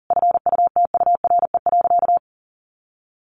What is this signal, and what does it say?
Morse code, heard as a keyed tone, about 40 wpm, 700 Hz. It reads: FUTURE.
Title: Sentences from Top 1500 Words 40wpm